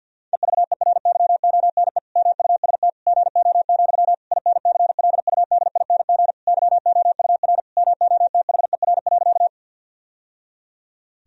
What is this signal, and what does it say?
Morse code, heard as a keyed tone, about 50 wpm, 700 Hz. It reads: EVERYONE MUST GO, INCLUDING YOUR MOTHER.
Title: Sentences from Top 1000 Words - Review Part 4 - 50wpm